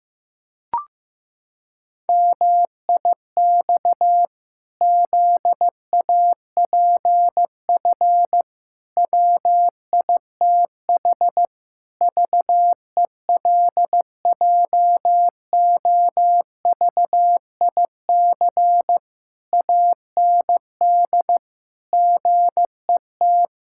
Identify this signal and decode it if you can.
Morse code, heard as a keyed tone, about 15 wpm, 700 Hz. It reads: MIX ZAPF WITH VELJOVIC AND GET
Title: Pangram Challenge v2 - 15wpm